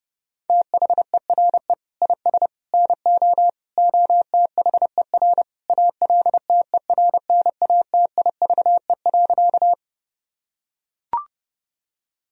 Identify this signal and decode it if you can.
Morse code, heard as a keyed tone, about 32 wpm, 700 Hz. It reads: THERE IS NO OTHER ALTERNATIVE.
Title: Sentences from Top 1000 Words - Review Part 4 - 30wpm